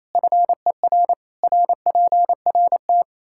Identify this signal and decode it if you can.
Morse code, heard as a keyed tone, about 27 wpm, 700 Hz. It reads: FER RPRT
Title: K1USN SST QSO Practice 28wpm